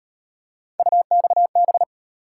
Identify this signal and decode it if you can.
Morse code, heard as a keyed tone, about 36 wpm, 700 Hz. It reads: UXB